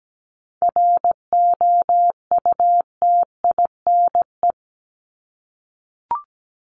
Morse code, heard as a keyed tone, about 17 wpm, 700 Hz. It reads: ROUTINE
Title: Top 1501-1750 Words 17wpm